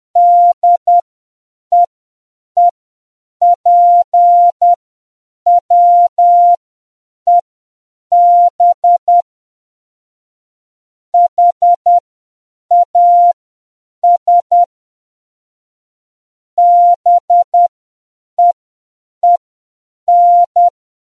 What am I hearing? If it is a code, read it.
Morse code, heard as a keyed tone, about 10 wpm, spaced as for 7 wpm, 700 Hz. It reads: DEEPWEB HAS BEEN